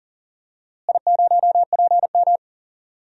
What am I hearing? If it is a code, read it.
Morse code, heard as a keyed tone, about 40 wpm, 700 Hz. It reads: I0PM
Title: DXCC-100 Call Signs - 4 Characters - Repeated 2x - 40wpm